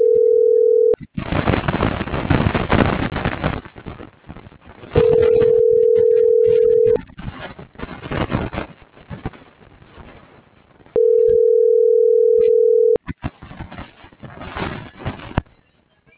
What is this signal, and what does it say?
Ringback tone (North America), 440 + 480 Hz.
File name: CENTRAL BLN / out ericka.canales - .71414